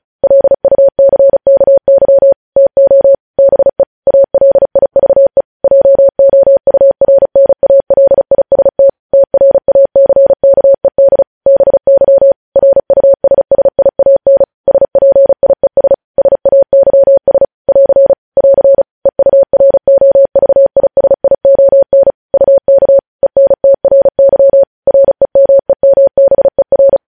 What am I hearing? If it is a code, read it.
Morse code, heard as a keyed tone, about 36 wpm, 550 Hz. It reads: LUCKY TO BE ALIVE JOURNALIST TRACKED BY RUSSIAN SPIES SAYS <AR> <AR> EUROVISION UK ENTRY REMEMBER
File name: podcast / News headlines-1x-35x35